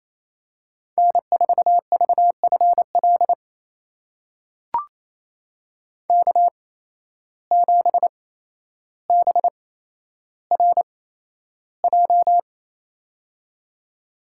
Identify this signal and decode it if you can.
Morse code, heard as a keyed tone, about 28 wpm, spaced as for 12 wpm, 700 Hz. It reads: N4VFL K7BRJ
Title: USA Call Signs - 5 Characters - 8x Character Spacing - 28wpm